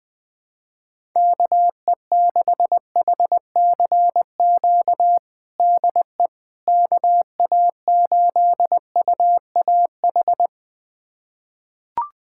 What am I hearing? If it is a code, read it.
Morse code, heard as a keyed tone, about 20 wpm, 700 Hz. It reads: KE6HCQ DE KA8UAH
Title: USA Callsign DE Callsign 20wpm